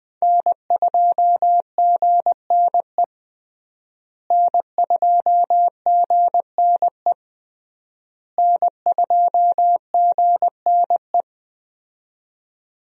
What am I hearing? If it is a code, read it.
Morse code, heard as a keyed tone, about 20 wpm, 700 Hz. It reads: N2GNE N2GNE N2GNE
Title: USA Call Signs - 5 Characters - Repeated 3x - 20wpm